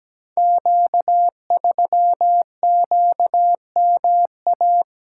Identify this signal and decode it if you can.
Morse code, heard as a keyed tone, about 17 wpm, 700 Hz. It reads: Q3QMA
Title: Code Groups - 5 Letter:Numbers - 17wpm